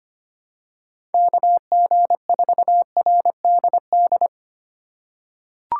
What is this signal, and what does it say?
Morse code, heard as a keyed tone, about 25 wpm, 700 Hz. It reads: KG4RDD
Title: USA Call Signs - 6 Characters - 4x Character Spacing - 25wpm